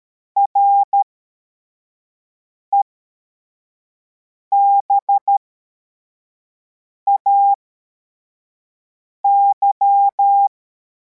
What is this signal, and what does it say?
Morse code, heard as a keyed tone, about 13 wpm, spaced as for 4 wpm, 800 Hz. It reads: REBAY
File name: formation / cw / Text63